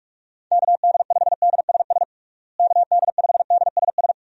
Morse code, heard as a keyed tone, about 43 wpm, 700 Hz. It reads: KD5DSS KD5DSS
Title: USA Call Signs - 6 Characters - Repeated 2x - 45wpm